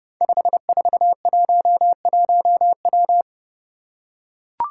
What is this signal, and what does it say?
Morse code, heard as a keyed tone, about 30 wpm, 700 Hz. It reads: 5411W
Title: Five Challenging Letter-Number Combinations 30wpm